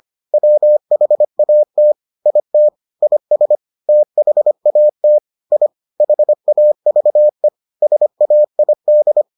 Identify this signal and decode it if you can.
Morse code, heard as a keyed tone, about 25 wpm, 600 Hz. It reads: WHAT IT IS THAT I HAVE SAID